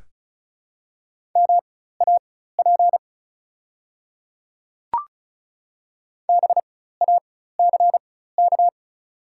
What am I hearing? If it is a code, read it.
Morse code, heard as a keyed tone, about 34 wpm, spaced as for 16 wpm, 700 Hz. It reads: MAP BACK
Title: 4x Character Spacing - Top 500 Words - 35wpm